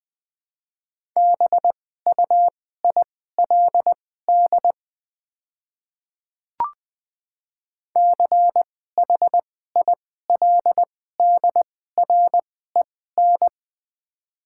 Morse code, heard as a keyed tone, about 20 wpm, spaced as for 14 wpm, 700 Hz. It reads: BUILD CHILDREN
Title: 2x Character Spacing - Top 300 Words - 20wpm